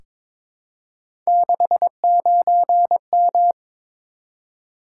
Morse code, heard as a keyed tone, about 22 wpm, 700 Hz. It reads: 69M
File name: practice / 22wpm / Three Letter-Number Combinations - 22wpm